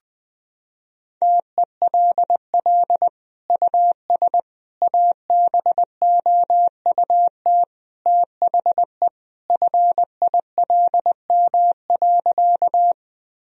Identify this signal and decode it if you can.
Morse code, heard as a keyed tone, about 20 wpm, 700 Hz. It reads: TELL US ABOUT THE FILM.
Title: Sentences from Top 1000 Words - Review Part 3 - 20wpm